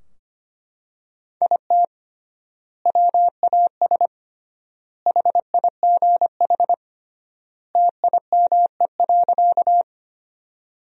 Morse code, heard as a keyed tone, about 25 wpm, 700 Hz. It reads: IT WAS HIGH TIME.
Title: 3x Word-Spacing - Sentences from Top 200 Words - 25wpm